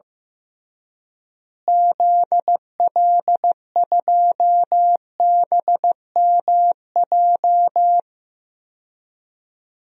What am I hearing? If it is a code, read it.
Morse code, heard as a keyed tone, about 15 wpm, 700 Hz. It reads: ZL2BMJ
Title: DXCC-100 Call Signs - 6 Characters - Repeated 3x - 15wpm